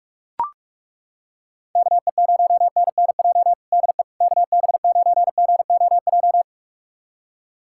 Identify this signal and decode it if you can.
Morse code, heard as a keyed tone, about 42 wpm, 690 Hz. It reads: KE0NNJ DE KB9GOJ
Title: USA Callsign DE Callsign - 6 Characters - 45wpm